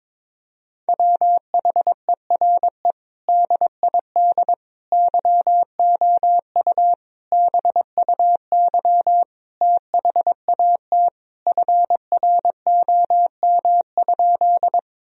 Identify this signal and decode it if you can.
Morse code, heard as a keyed tone, about 22 wpm, 700 Hz. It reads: WHERE DID YOU BUY THAT FROM?
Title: Sentences from Top 1000 Words - Review Part 4 - 22wpm